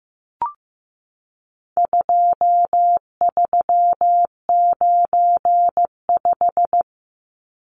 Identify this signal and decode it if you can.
Morse code, heard as a keyed tone, about 15 wpm, 700 Hz. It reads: 2395
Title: Numbers - 4-Digit 15wpm